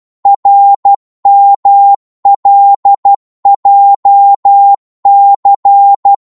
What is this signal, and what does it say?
Morse code, heard as a keyed tone, about 12 wpm, 800 Hz. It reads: RMLJC
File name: Morse / ltr-12wpm